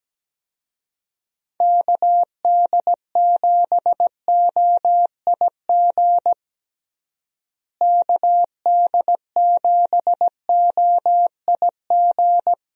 Morse code, heard as a keyed tone, about 17 wpm, 700 Hz. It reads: KD7OIG KD7OIG
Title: USA Call Signs - 6 Characters - Repeated 3x - 17wpm